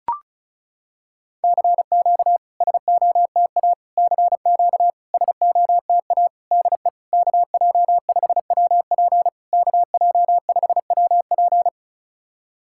Morse code, heard as a keyed tone, about 36 wpm, 700 Hz. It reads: CQ SOTA CQ SOTA DE KJ5WP KJ5WP
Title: SOTA QSO Practice 35wpm